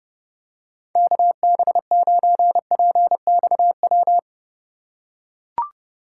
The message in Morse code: KB9PXW

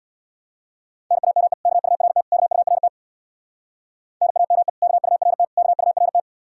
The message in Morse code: NAME KURT KURT NAME KURT KURT

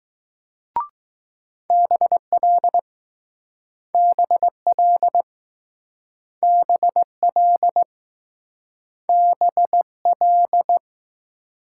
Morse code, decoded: BL BL BL BL

